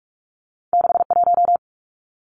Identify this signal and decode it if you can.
Morse code, heard as a keyed tone, about 46 wpm, 700 Hz. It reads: 61
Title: Numbers - 2-Digit 45wpm